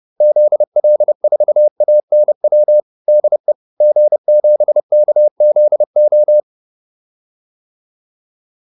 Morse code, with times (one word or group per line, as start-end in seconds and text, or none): 0.2-2.8 ZL4ANW
3.1-3.5 DE
3.8-6.4 G7KZO